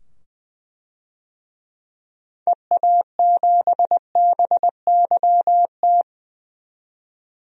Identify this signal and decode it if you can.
Morse code, heard as a keyed tone, about 20 wpm, 700 Hz. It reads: EA7BYT
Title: DXCC-100 Call Signs - 6 Characters - 20wpm